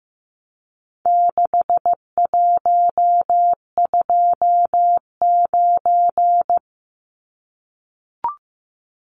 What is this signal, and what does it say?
Morse code, heard as a keyed tone, about 15 wpm, 700 Hz. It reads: 6129
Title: Numbers - 4-Digit 15wpm